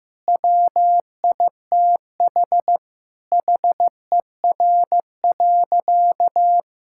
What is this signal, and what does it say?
Morse code, heard as a keyed tone, about 16 wpm, 700 Hz. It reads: WITH HER.
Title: Sentences from Top 1000 Words - Review Part 6 - 15wpm